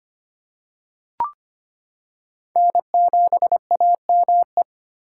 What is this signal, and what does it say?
Morse code, heard as a keyed tone, about 25 wpm, 700 Hz. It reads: N7AME
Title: K1USN SST QSO Practice - 25wpm